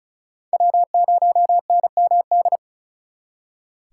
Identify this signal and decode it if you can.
Morse code, heard as a keyed tone, about 44 wpm, 700 Hz. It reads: W0NMD
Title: Speed Racing - USA 1x3 Call Signs 35wpm